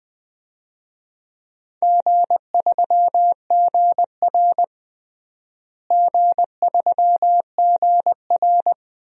Morse code, heard as a keyed tone, about 20 wpm, 700 Hz. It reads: G3GR G3GR
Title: DXCC-100 Call Signs - 4 Characters - Repeated 2x - 20wpm